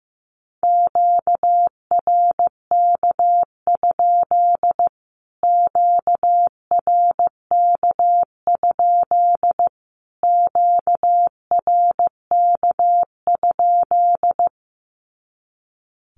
Morse code, heard as a keyed tone, about 15 wpm, 700 Hz. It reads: QRK? QRK? QRK?
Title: Q-Codes Repeated 3x 15wpm